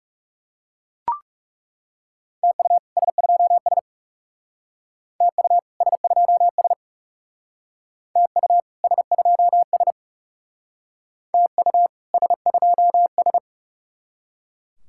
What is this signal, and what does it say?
Morse code, heard as a keyed tone, about 38 wpm, 700 Hz. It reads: TU S2S TU S2S TU S2S TU S2S